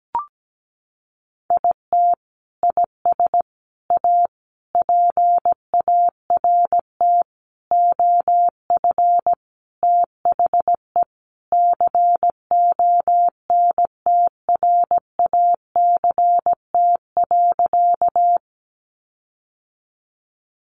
Morse code, heard as a keyed tone, about 17 wpm, 700 Hz. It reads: IT IS A PART OF THE CONTRACT.
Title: Sentences from Top 1400 Words 17wpm